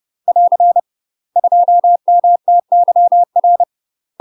Morse code, heard as a keyed tone, about 30 wpm, 700 Hz. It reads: <AR> 2MTYR
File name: UK MASTER QSOs TRAINING / 30WPM / UK MASTER QSOs 0048